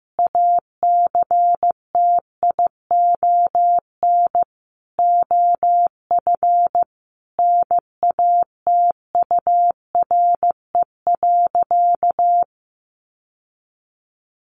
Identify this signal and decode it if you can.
Morse code, heard as a keyed tone, about 15 wpm, 700 Hz. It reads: ACTION OF NATURE.
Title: Sentences from Top 1200 Words 15wpm